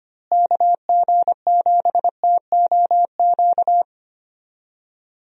Morse code, heard as a keyed tone, about 25 wpm, 700 Hz. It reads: KG7TOQ
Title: USA Call Signs - 6 Characters - Repeated 3x - 25wpm